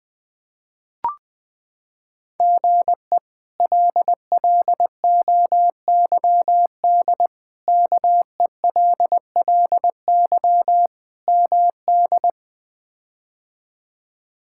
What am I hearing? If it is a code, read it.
Morse code, heard as a keyed tone, about 20 wpm, 700 Hz. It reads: GE LLOYD KELLY MD
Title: K1USN SST QSO Practice - 20wpm